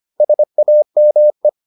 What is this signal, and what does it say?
Morse code, heard as a keyed tone, about 25 wpm, 600 Hz. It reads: SAME